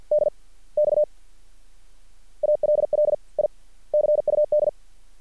Morse code, heard as a keyed tone, about 50 wpm, spaced as for 34 wpm, 600 Hz. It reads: DX ALL I CUD
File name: archived / CRSnet